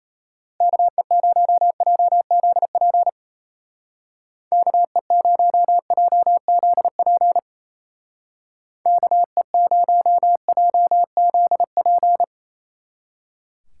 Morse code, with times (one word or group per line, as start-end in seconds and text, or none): 0.6-3.1 KE0JZP
4.5-7.4 KE0JZP
8.9-12.2 KE0JZP